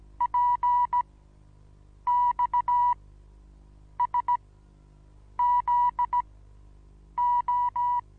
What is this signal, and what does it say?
Morse code, heard as a keyed tone, about 16 wpm, spaced as for 7 wpm, 970 Hz. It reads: PXSZO
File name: Onlinelehrgang / morsen / DJ4UF Morsen25